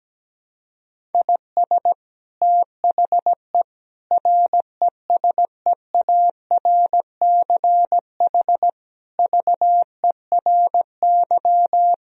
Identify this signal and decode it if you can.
Morse code, heard as a keyed tone, about 17 wpm, 700 Hz. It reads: IS THE RESEARCH VERY
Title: Sentences from Top 1000 Words - Review Part 4 - 17wpm